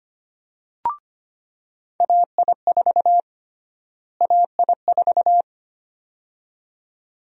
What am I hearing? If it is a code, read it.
Morse code, heard as a keyed tone, about 25 wpm, 700 Hz. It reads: AI4 AI4
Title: USA Call Sign Prefixes - Repeated 2x - 25wpm